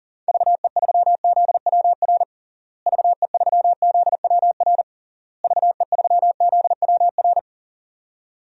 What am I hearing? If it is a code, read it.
Morse code, heard as a keyed tone, about 40 wpm, 700 Hz. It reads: VE3ZWR VE3ZWR VE3ZWR
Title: DXCC-100 Call Signs - 6 Characters - Repeated 3x - 40wpm